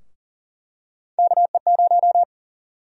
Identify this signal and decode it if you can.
Morse code, heard as a keyed tone, about 37 wpm, 700 Hz. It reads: KE0
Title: USA Call Sign Prefixes - Repeated 2x - 40wpm